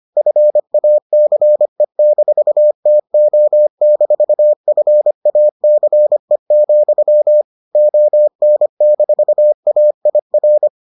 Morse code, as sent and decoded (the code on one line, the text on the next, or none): ..-. .- -.-. -....- - --- -....- ..-. .- -.-. --..-- / --- -. -....- .- .. .-.
FACE-TO-FACE, ON-AIR